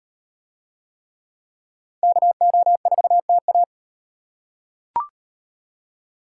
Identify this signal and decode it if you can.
Morse code, heard as a keyed tone, about 35 wpm, 700 Hz. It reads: KO4TA